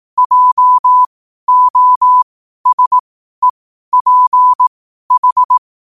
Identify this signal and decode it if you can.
Morse code, heard as a keyed tone, about 18 wpm, spaced as for 12 wpm, 1000 Hz. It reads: JOSEPH
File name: qotdfiles / 12wpm